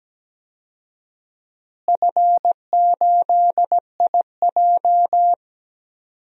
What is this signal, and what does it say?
Morse code, heard as a keyed tone, about 17 wpm, 700 Hz. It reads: F8IJ
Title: DXCC-100 Call Signs - 4 Characters - Repeated 3x - 17wpm